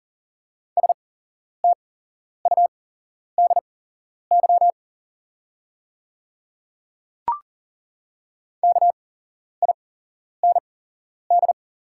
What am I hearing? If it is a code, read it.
Morse code, heard as a keyed tone, about 40 wpm, spaced as for 11 wpm, 700 Hz. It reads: STUDY KIND
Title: 8x Character Spacing - Top 300 Words - 40wpm